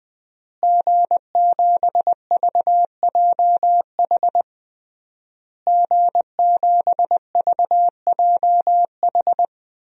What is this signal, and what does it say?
Morse code, heard as a keyed tone, about 20 wpm, 700 Hz. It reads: G7VJH G7VJH
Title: DXCC-100 Call Signs - 5 Characters - Repeated 3x - 20wpm